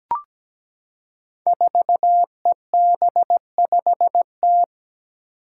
Morse code, heard as a keyed tone, about 17 wpm, 700 Hz. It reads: 4EB5T
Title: Five Challenging Letter-Number Combinations 17wpm